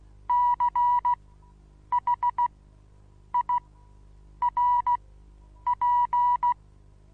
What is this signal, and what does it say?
Morse code, heard as a keyed tone, about 16 wpm, spaced as for 8 wpm, 970 Hz. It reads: CHIRP
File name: Onlinelehrgang / morsen / DJ4UF Morsen25